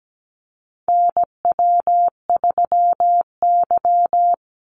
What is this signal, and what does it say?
Morse code, heard as a keyed tone, about 17 wpm, 700 Hz. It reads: NW3Y
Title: Call Signs - 17wpm